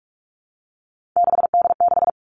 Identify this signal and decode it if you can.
Morse code, heard as a keyed tone, about 46 wpm, 700 Hz. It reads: 6D6
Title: Three Challenging Letter-Number Combinations 45wpm